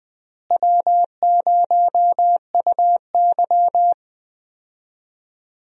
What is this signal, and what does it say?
Morse code, heard as a keyed tone, about 20 wpm, 700 Hz. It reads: W0UY
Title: USA Call Signs - 4 Characters - Repeated 3x - 20wpm